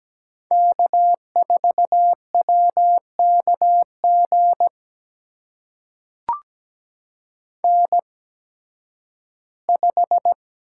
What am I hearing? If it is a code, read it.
Morse code, heard as a keyed tone, about 17 wpm, spaced as for 11 wpm, 700 Hz. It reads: K4WKG N5
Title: USA Call Signs - 5 Characters - 8x Character Spacing - 17wpm